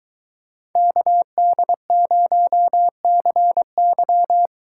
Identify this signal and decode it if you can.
Morse code, heard as a keyed tone, about 22 wpm, 700 Hz. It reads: KD0CY